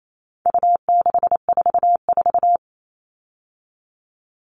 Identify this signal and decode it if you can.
Morse code, heard as a keyed tone, about 28 wpm, 700 Hz. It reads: U644
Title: All Cut Numbers - 4-Digit 28wpm